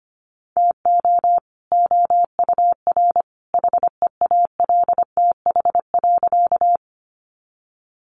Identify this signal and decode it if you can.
Morse code, heard as a keyed tone, about 25 wpm, 700 Hz. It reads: TO OUR HEALTH.
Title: Sentences from Top 600 Words 25wpm